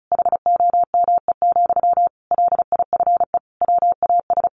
Morse code, heard as a keyed tone, about 34 wpm, 700 Hz. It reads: HOME, LIFE WAS
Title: Sentences from Top 1400 Words 35wpm